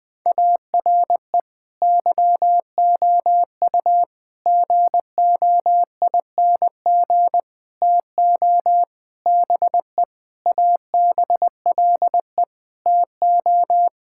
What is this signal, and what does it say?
Morse code, heard as a keyed tone, about 20 wpm, 700 Hz. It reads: ARE YOU GOING TO BE ABLE TO
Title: Sentences with 5-Word Phrases - Spoken Beginning - 20wpm